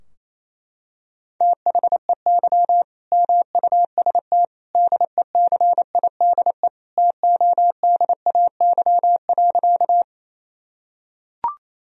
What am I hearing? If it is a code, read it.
Morse code, heard as a keyed tone, about 28 wpm, 700 Hz. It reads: THEY MUST DECIDE TODAY.
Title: Sentences from Top 1000 Words - Review Part 3 - 28wpm